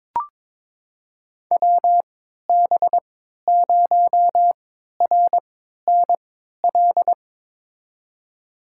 Morse code, heard as a keyed tone, about 21 wpm, spaced as for 12 wpm, 700 Hz. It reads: WB0RNL
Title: USA Call Signs - 3x Character Spacing - 22wpm